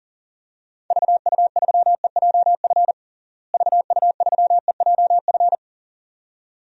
Morse code, heard as a keyed tone, about 40 wpm, 700 Hz. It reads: VU3EJF VU3EJF
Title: DXCC-100 Call Signs - 6 Characters - Repeated 2x - 40wpm